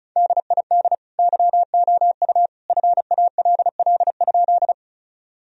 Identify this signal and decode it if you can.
Morse code, heard as a keyed tone, about 36 wpm, 700 Hz. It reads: DID YOU FALL?
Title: Sentences from Top 1000 Words - Review Part 2 - 35wpm